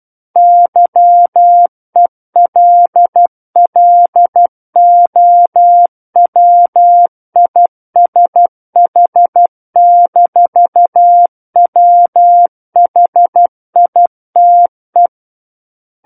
Morse code, heard as a keyed tone, about 12 wpm, 700 Hz. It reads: YELLOWISH-WHITE